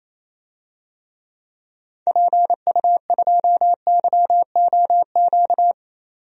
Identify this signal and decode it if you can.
Morse code, heard as a keyed tone, about 28 wpm, 700 Hz. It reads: PU2YOQ
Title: DXCC-100 Call Signs - 6 Characters - 28wpm